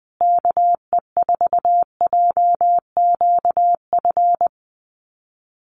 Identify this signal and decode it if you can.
Morse code, heard as a keyed tone, about 20 wpm, 700 Hz. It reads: KE4JQF
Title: USA Call Signs Repeated 3x 20wpm